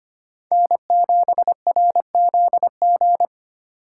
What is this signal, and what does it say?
Morse code, heard as a keyed tone, about 32 wpm, 700 Hz. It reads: N7RZG